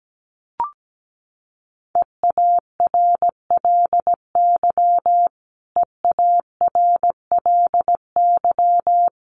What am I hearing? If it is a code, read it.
Morse code, heard as a keyed tone, about 17 wpm, 700 Hz. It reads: EARLY EARLY
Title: Top 500 Words Repeated 17wpm